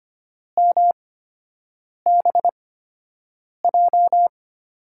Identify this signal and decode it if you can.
Morse code, heard as a keyed tone, about 25 wpm, spaced as for 7 wpm, 700 Hz. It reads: MBJ
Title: USA Call Signs - 8x Character Spacing - 25wpm